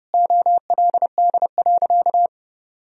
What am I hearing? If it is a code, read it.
Morse code, heard as a keyed tone, about 32 wpm, 700 Hz. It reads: OLD.
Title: Sentences from Top 1000 Words - Review Part 2 - 30wpm